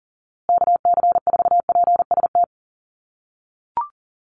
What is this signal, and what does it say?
Morse code, heard as a keyed tone, about 40 wpm, 700 Hz. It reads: KC4PST
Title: USA Call Signs Repeated 3x 40wpm